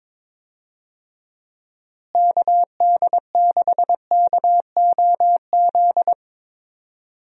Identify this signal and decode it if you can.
Morse code, heard as a keyed tone, about 23 wpm, 700 Hz. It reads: KD6KOZ
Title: USA Call Signs - 6 Characters - Repeated 2x - 22wpm